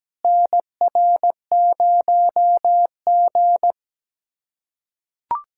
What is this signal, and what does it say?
Morse code, heard as a keyed tone, about 17 wpm, 700 Hz. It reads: NR0G
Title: USA Callsign DE Callsign - 4 Characters - 17wpm